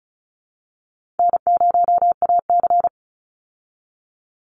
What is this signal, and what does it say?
Morse code, heard as a keyed tone, about 34 wpm, 700 Hz. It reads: N0AC
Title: Easy Call Signs Repeated 3x - 35wpm